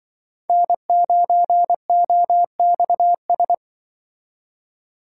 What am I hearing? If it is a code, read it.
Morse code, heard as a keyed tone, about 23 wpm, 700 Hz. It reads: N9OXS
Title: Speed Racing - USA 1x3 Call Signs 20wpm